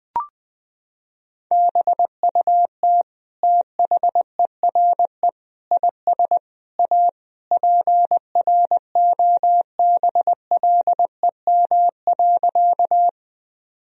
Morse code, heard as a keyed tone, about 20 wpm, 700 Hz. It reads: BUT THERE IS A PROBLEM.
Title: Sentences from Top 1000 Words - Review Part 2 - 20wpm